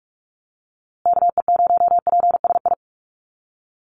Morse code, heard as a keyed tone, about 44 wpm, 700 Hz. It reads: KE0PSI
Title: USA Call Signs Repeated 2x 45wpm